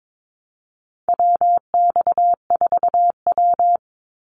Morse code, heard as a keyed tone, about 22 wpm, 700 Hz. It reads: WX4W
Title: Call Signs Repeated 3x - 22wpm